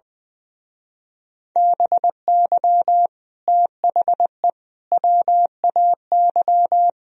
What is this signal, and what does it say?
Morse code, heard as a keyed tone, about 22 wpm, 700 Hz. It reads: BY THE WAY
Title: Speed-Racing - Sets of 3 Words 20wpm